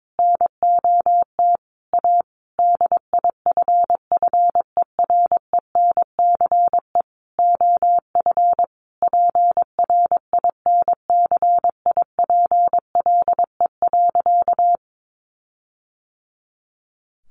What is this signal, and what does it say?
Morse code, heard as a keyed tone, about 22 wpm, 700 Hz. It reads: NOT A DIFFERENCE OF PRINCIPLE.
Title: Sentences from Top 1400 Words 22wpm